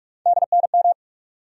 Morse code, heard as a keyed tone, about 47 wpm, 700 Hz. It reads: DNM
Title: USA Call Sign Postfixes - 3 Letters 45wpm